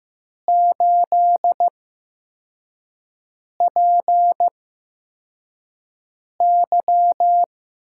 Morse code, heard as a keyed tone, about 15 wpm, spaced as for 4 wpm, 700 Hz. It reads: 8PY
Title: USA Call Signs - 4 Characters - 8x Character Spacing - 15wpm